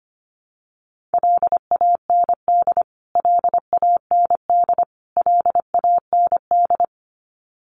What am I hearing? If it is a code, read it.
Morse code, heard as a keyed tone, about 25 wpm, 700 Hz. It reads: LAND LAND LAND
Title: Top 200 Words Repeated 25wpm